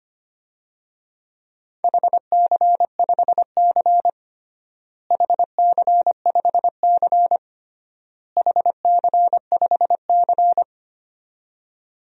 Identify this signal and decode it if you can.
Morse code, heard as a keyed tone, about 25 wpm, 700 Hz. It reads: HC5C HC5C HC5C
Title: DXCC-100 Call Signs - 4 Characters - Repeated 3x - 25wpm